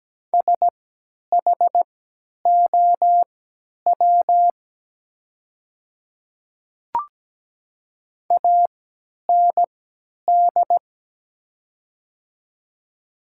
Morse code, heard as a keyed tone, about 17 wpm, spaced as for 10 wpm, 700 Hz. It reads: SHOW AND
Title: 3x Character Spacing - Top 200 Words - 17wpm